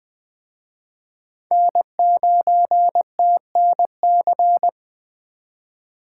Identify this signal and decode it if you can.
Morse code, heard as a keyed tone, about 23 wpm, 700 Hz. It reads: N9TNC
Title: Speed Racing - USA 1x3 Call Signs 20wpm